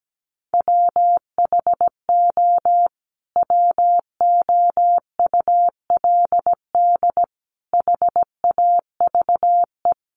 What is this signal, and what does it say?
Morse code, heard as a keyed tone, about 17 wpm, 700 Hz. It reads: WHO WOULD HAVE
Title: Sentences from Top 1200 Words 17wpm